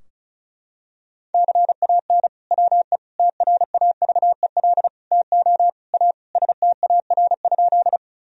Morse code, heard as a keyed tone, about 34 wpm, 700 Hz. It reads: CAN WE TRAVEL TO A STAR?
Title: Sentences from Top 1000 Words - Review Part 2 - 35wpm